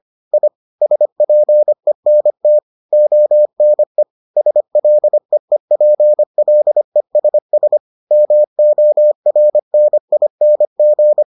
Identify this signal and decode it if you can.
Morse code, heard as a keyed tone, about 25 wpm, 600 Hz. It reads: I SPENT ONE SLEEPLESS MORNING